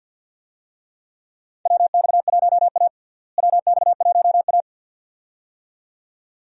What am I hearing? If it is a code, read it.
Morse code, heard as a keyed tone, about 50 wpm, 700 Hz. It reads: WX1A WX1A
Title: USA Call Signs - 4 Characters - Repeated 2x - 50wpm